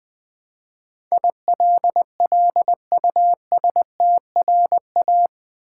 Morse code, heard as a keyed tone, about 20 wpm, 700 Hz. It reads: ILLUSTRA
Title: Build A Word - 4 Letters At A Time - 12-Letter Words - 20wpm